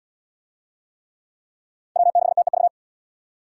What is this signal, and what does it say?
Morse code, heard as a keyed tone, about 66 wpm, 700 Hz. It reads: W6TEV